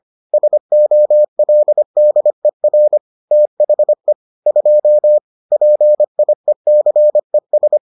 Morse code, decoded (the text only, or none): SOLDER THE 2 PIECES